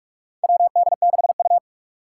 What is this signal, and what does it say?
Morse code, heard as a keyed tone, about 46 wpm, 700 Hz. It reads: WD6U
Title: USA Call Signs - 4 Characters - 3x Character Spacing - 45wpm